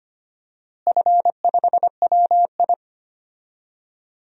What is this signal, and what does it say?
Morse code, heard as a keyed tone, about 25 wpm, 700 Hz. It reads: F5WI
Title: DXCC-100 Call Signs - 4 Characters - Repeated 3x - 25wpm